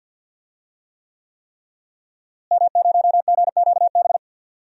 Morse code, heard as a keyed tone, about 50 wpm, 700 Hz. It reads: M0GQB
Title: DXCC-100 Call Signs - 5 Characters - Repeated 3x - 50wpm